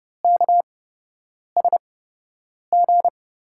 Morse code, heard as a keyed tone, about 31 wpm, spaced as for 8 wpm, 700 Hz. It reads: KSG
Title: USA Call Signs - 6 Characters - 8x Character Spacing - 30wpm